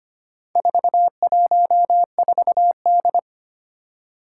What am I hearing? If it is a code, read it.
Morse code, heard as a keyed tone, about 25 wpm, 700 Hz. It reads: 414D
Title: Four Challenging Letter-Number Combinations 25wpm